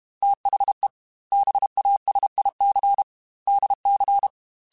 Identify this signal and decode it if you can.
Morse code, heard as a keyed tone, about 32 wpm, 800 Hz. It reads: THE BASIC DC
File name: Technical Subjects / Technical Subjects-mp3s / Sw Pwr Sup @30